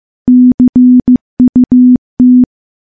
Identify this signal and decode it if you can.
Morse code, heard as a keyed tone, about 15 wpm, 250 Hz. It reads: CUT
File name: morse / morse-training-15WPM